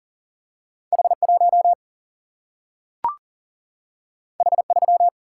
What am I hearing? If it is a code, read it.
Morse code, heard as a keyed tone, about 40 wpm, 700 Hz. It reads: H 1 H 3
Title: Single-Letter Single-Number 40wpm